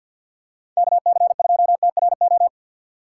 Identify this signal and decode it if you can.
Morse code, heard as a keyed tone, about 47 wpm, 700 Hz. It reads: KC2TRO
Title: USA Call Signs - 6 Characters - 6x Character Spacing - 50wpm